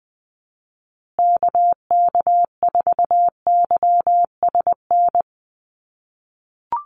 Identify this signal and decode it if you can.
Morse code, heard as a keyed tone, about 20 wpm, 700 Hz. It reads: KK4YSN
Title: USA Call Signs Repeated 2x 20wpm